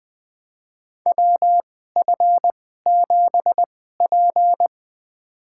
Morse code, heard as a keyed tone, about 20 wpm, spaced as for 15 wpm, 700 Hz. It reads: WF7P
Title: USA Call Signs - 2x Character Spacing - 20wpm